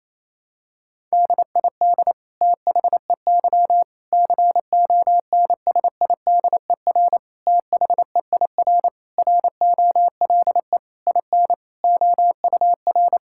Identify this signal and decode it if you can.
Morse code, heard as a keyed tone, about 28 wpm, 700 Hz. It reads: DID THEY CONSIDER THEIR ROLE IN OUR
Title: Sentences from Top 1000 Words - Review Part 6 - 28wpm